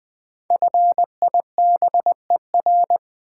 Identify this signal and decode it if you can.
Morse code, heard as a keyed tone, about 20 wpm, 690 Hz. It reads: FIBER
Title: 5 Letter Words - First 3 Letters Given - 20wpm